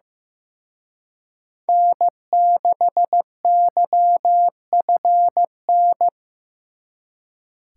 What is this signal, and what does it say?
Morse code, heard as a keyed tone, about 18 wpm, 700 Hz. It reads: N6YFN
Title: Speed Racing - USA 1x3 Call Signs 15wpm